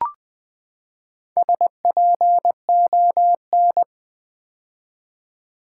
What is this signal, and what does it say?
Morse code, heard as a keyed tone, about 20 wpm, 700 Hz. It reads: SPON
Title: Build A Word - 4 Letters At A Time - 11-Letter Words - 20wpm